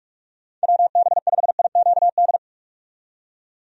Morse code, heard as a keyed tone, about 41 wpm, 700 Hz. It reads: WB5IQD
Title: USA Call Signs - 6 Characters - 45wpm